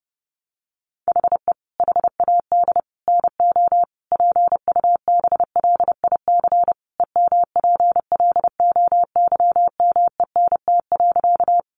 Morse code, heard as a keyed tone, about 30 wpm, 700 Hz. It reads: HE HAD NO PUBLIC EMPLOYMENT.
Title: Sentences from Top 1500 Words 30wpm